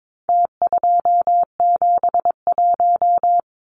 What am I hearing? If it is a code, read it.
Morse code, heard as a keyed tone, about 22 wpm, 700 Hz. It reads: T271
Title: Numbers - Contest Style - 4-Digit 22wpm